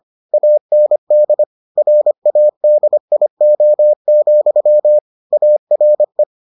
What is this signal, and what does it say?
Morse code, heard as a keyed tone, about 25 wpm, 600 Hz. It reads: AND RADIO, ARE